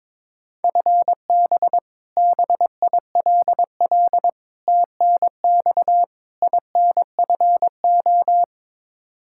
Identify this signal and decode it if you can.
Morse code, heard as a keyed tone, about 22 wpm, 700 Hz. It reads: FB BILL TNX INFO
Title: Practice QSO - Exchange 3 - More Info - 22wpm